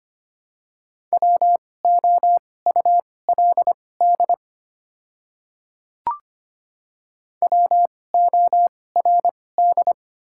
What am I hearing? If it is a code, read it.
Morse code, heard as a keyed tone, about 25 wpm, spaced as for 18 wpm, 690 Hz. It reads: WOULD WORD
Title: 2x Character Spacing - Top 100 Words - 25wpm